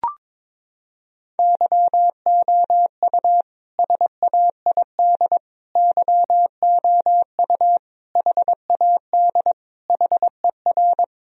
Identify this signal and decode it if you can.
Morse code, heard as a keyed tone, about 22 wpm, 700 Hz. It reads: YOU SAID YOU HAD HER
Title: Sentences from Top 1000 Words - Review Part 1 - 22wpm